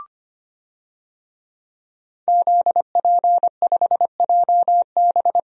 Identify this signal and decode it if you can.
Morse code, heard as a keyed tone, about 25 wpm, 700 Hz. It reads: ZP5JB
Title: DXCC-100 Call Signs - 5 Characters - Repeated 2x - 25wpm